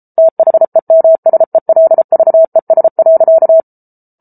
Morse code, heard as a keyed tone, about 33 wpm, 650 Hz. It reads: THEMSELVES.